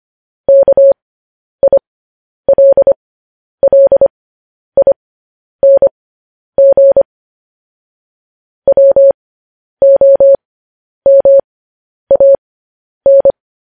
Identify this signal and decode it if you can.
Morse code, heard as a keyed tone, about 25 wpm, spaced as for 10 wpm, 550 Hz. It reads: KILLING WOMAN